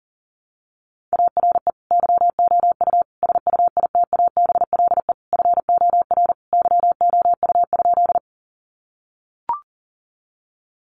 Morse code, heard as a keyed tone, about 40 wpm, 700 Hz. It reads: ARE YOU SUITABLE FOR YOU?
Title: Sentences from Top 1400 Words 40wpm